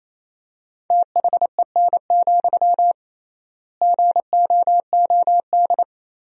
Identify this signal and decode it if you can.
Morse code, heard as a keyed tone, about 27 wpm, 700 Hz. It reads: THEN, GOOD
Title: 3x Word-Spacing - Sentences from Top 300 Words - 28wpm